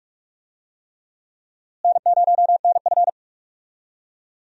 Morse code, heard as a keyed tone, about 42 wpm, 700 Hz. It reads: N0NF
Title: USA Call Signs - 4 Characters - Repeated 3x - 45wpm